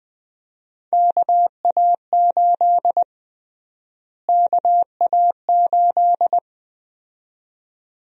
Morse code, heard as a keyed tone, about 20 wpm, 700 Hz. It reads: KA8 KA8
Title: USA Call Sign Prefixes - Repeated 2x - 20wpm